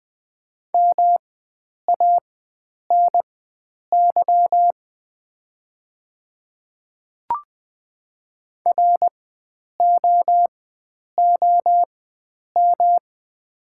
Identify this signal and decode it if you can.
Morse code, heard as a keyed tone, about 20 wpm, spaced as for 9 wpm, 690 Hz. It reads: MANY ROOM
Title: 4x Character Spacing - Top 500 Words - 20wpm